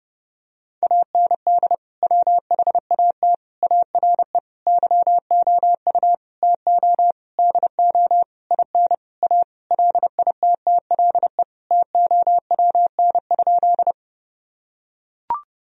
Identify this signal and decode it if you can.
Morse code, heard as a keyed tone, about 30 wpm, 700 Hz. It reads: AND WHAT ARE YOU TO DO IN A LITTLE TOWN?
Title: Sentences from Top 1000 Words - Review Part 2 - 30wpm